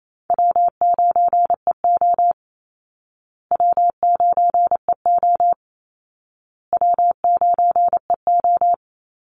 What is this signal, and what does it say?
Morse code, heard as a keyed tone, about 28 wpm, 700 Hz. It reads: W9EO W9EO W9EO
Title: Call Signs Repeated 2x - 28wpm